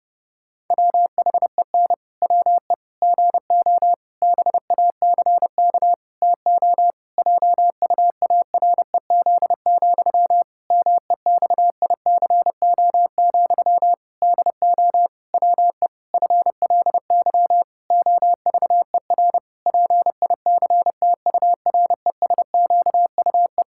Morse code, heard as a keyed tone, about 29 wpm, 700 Hz. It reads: WHEN WE GO BACK TO JUAREZ, MEXICO, DO WE FLY OVER PICTURESQUE
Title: Pangram Challenge v2 - 30wpm